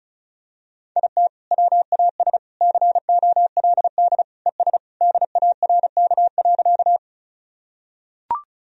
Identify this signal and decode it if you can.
Morse code, heard as a keyed tone, about 34 wpm, 700 Hz. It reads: IT WAS COLD ES DARK.
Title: Sentences from Top 500 Words and Abbreviations 35wpm